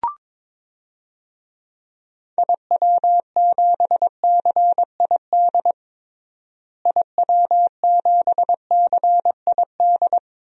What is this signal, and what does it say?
Morse code, heard as a keyed tone, about 22 wpm, 700 Hz. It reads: IW7CID IW7CID
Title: DXCC-100 Call Signs - 6 Characters - Repeated 2x - 22wpm